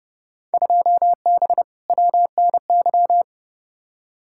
Morse code, heard as a keyed tone, about 30 wpm, 700 Hz. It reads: 2B WNY
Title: Field Day Exchanges Encore - 30wpm